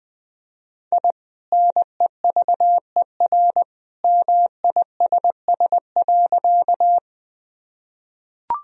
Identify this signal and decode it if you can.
Morse code, heard as a keyed tone, about 20 wpm, 700 Hz. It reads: I NEVER MISS.
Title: Sentences from Top 1000 Words - Review Part 3 - 20wpm